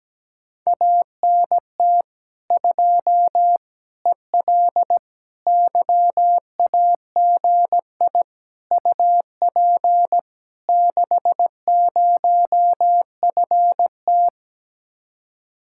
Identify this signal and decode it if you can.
Morse code, heard as a keyed tone, about 17 wpm, 700 Hz. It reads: ANT 2 EL YAGI UP 60FT
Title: Speed-Racing Common QSO Phrases - 17wpm